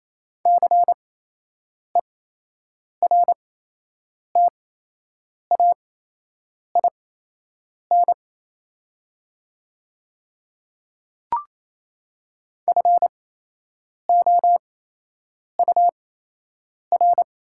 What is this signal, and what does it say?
Morse code, heard as a keyed tone, about 28 wpm, spaced as for 8 wpm, 700 Hz. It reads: CERTAIN FOUR
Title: 8x Character Spacing - Top 500 Words - 28wpm